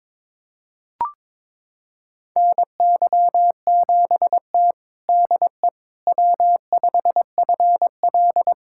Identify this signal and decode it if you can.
Morse code, heard as a keyed tone, about 21 wpm, 700 Hz. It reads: NY7T DE W5FL
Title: USA Callsign DE Callsign - 4 Characters - 22wpm